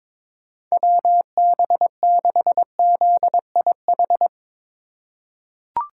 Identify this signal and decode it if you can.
Morse code, heard as a keyed tone, about 21 wpm, 700 Hz. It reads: WB6ZIH
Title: USA Call Signs - 6 Characters - 4x Character Spacing - 22wpm